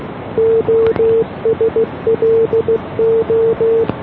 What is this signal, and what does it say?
Morse code, heard as a keyed tone, about 15 wpm, 470 Hz. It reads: OSLO